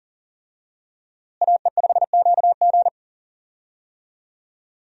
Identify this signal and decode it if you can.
Morse code, heard as a keyed tone, about 40 wpm, 700 Hz. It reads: AE5QG
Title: USA Call Signs - 5 Characters - Repeated 3x - 40wpm